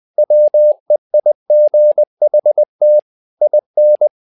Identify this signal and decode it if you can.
Morse code, heard as a keyed tone, about 20 wpm, 600 Hz. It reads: WEIGHT IN
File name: tower trouble 20wpm